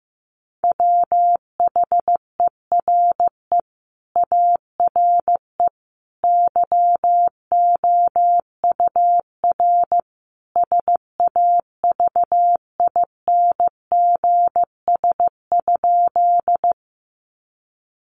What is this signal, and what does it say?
Morse code, heard as a keyed tone, about 15 wpm, 700 Hz. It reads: WHERE ARE YOUR SAVINGS?
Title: Sentences from Top 1200 Words 15wpm